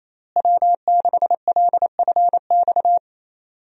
Morse code, heard as a keyed tone, about 28 wpm, 700 Hz. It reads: W6LFX
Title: USA Call Signs - 5 Characters - Repeated 2x - 28wpm